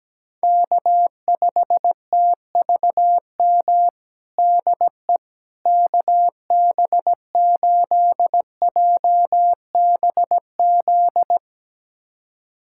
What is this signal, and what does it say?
Morse code, heard as a keyed tone, about 17 wpm, 700 Hz. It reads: K5TVM DE KB8JBZ
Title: USA Callsign DE Callsign 17wpm